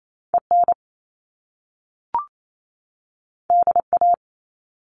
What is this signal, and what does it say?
Morse code, heard as a keyed tone, about 28 wpm, 700 Hz. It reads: EN DA
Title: Top Two Letter Combinations in Words 28wpm